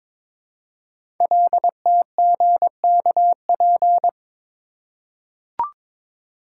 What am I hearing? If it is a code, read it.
Morse code, heard as a keyed tone, about 22 wpm, 700 Hz. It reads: LTGKP
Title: Code Groups - 5 Letter:Numbers - 22wpm